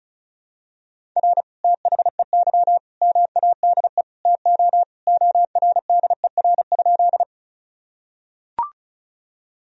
Morse code, heard as a keyed tone, about 36 wpm, 700 Hz. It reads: R THEY MADE TO ORDER?
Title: Sentences from Top 400 Words and Abbreviations 35wpm